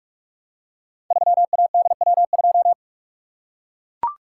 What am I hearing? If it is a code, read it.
Morse code, heard as a keyed tone, about 44 wpm, 700 Hz. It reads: 3ADW2
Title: Code Groups - 5 Letter:Numbers - 45wpm